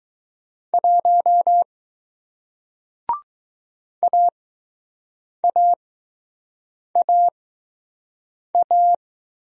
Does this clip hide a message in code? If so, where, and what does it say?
Yes: Morse code from 0.7 to 8.9 s, heard as a keyed tone, about 20 wpm, spaced as for 6 wpm, 700 Hz. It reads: 1 AAAA